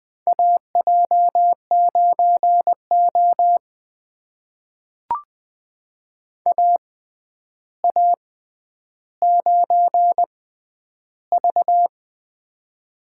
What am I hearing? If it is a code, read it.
Morse code, heard as a keyed tone, about 20 wpm, spaced as for 10 wpm, 700 Hz. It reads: AJ9O AA9V
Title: USA Call Signs - 4 Characters - 6x Character Spacing - 20wpm